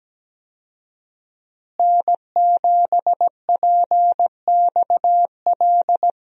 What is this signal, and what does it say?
Morse code, heard as a keyed tone, about 17 wpm, 700 Hz. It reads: N7PXL
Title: USA Call Signs - 5 Characters - Repeated 3x - 17wpm